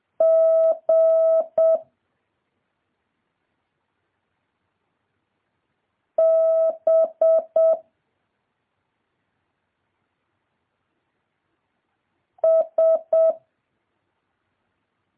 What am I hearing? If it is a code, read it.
Morse code, heard as a keyed tone, about 7 wpm, 650 Hz. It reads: G B S
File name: jh8cbh / 2023cw